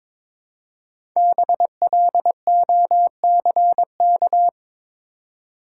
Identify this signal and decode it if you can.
Morse code, heard as a keyed tone, about 23 wpm, 700 Hz. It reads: BLOCK
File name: 5 Letter Words - First 3 Letters Given - 22wpm